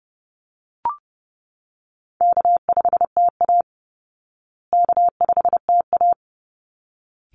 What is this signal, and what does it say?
Morse code, heard as a keyed tone, about 30 wpm, 700 Hz. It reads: K5TA K5TA
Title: Easy Call Signs Repeated 2x - 30wpm